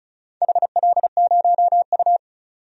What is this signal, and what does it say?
Morse code, heard as a keyed tone, about 36 wpm, 700 Hz. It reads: HL0U
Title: DXCC-100 Call Signs - 4 Characters - 35wpm